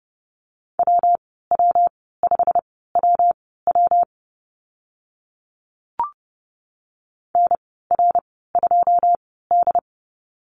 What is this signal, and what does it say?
Morse code, heard as a keyed tone, about 30 wpm, spaced as for 17 wpm, 700 Hz. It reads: WW5WW NR2D
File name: Lesson 19g - Letter w - Call Signs 30wpm